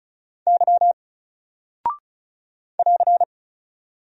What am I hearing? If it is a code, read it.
Morse code, heard as a keyed tone, about 36 wpm, 700 Hz. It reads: Y <AR>
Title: Single Character (Letter:Number:Punctuation:Prosign) Alt 35wpm